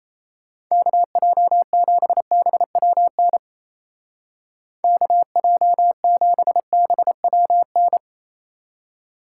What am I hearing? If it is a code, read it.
Morse code, heard as a keyed tone, about 34 wpm, 700 Hz. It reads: KJ7BWN KJ7BWN
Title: Speed Racing - USA 2x3 Call Signs 28wpm